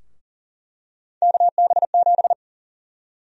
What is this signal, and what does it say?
Morse code, heard as a keyed tone, about 40 wpm, 700 Hz. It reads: KB7
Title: USA Call Sign Prefixes - Repeated 2x - 40wpm